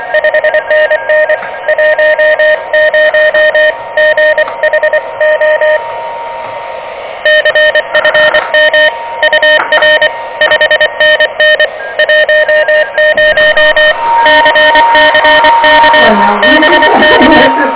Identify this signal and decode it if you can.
Morse code, heard as a keyed tone, about 24 wpm, 590 Hz, with noise. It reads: 5NN 10 GH O CFM UR 5NN 10 CCK BK
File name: OK QRP 2023 / OK QRP 2023